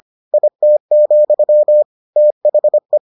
Morse code, heard as a keyed tone, about 25 wpm, 600 Hz. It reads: IT, THE